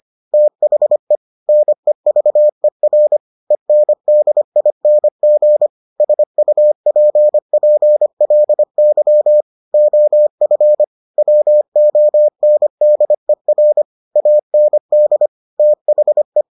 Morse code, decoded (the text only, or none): THE NEVER ENDING SUPPLY OF WONDER AND THE